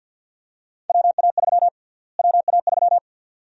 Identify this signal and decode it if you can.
Morse code, heard as a keyed tone, about 46 wpm, 700 Hz. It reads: WA3 WA3
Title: USA Call Sign Prefixes - Repeated 2x - 50wpm